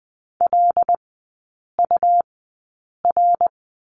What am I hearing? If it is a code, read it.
Morse code, heard as a keyed tone, about 20 wpm, spaced as for 8 wpm, 700 Hz. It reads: LUR
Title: 100 Lesson 17e - Letter u - Three Characters 20wpm